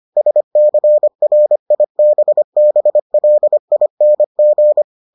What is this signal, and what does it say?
Morse code, heard as a keyed tone, about 25 wpm, 600 Hz. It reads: SCRIBBLING